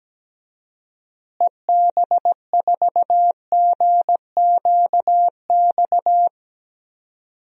Morse code, heard as a keyed tone, about 17 wpm, 700 Hz. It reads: EB4GQX
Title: DXCC-100 Call Signs - 6 Characters - Repeated 2x - 17wpm